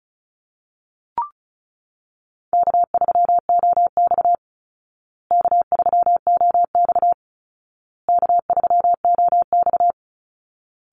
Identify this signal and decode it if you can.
Morse code, heard as a keyed tone, about 34 wpm, 700 Hz. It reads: K3OX K3OX K3OX
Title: Call Signs Repeated 2x - 35wpm